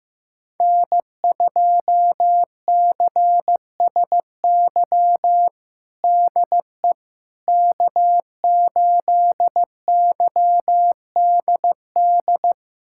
Morse code, heard as a keyed tone, about 15 wpm, 700 Hz. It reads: N2CSY DE K8YDD
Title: USA Callsign DE Callsign - 5 Characters - 15wpm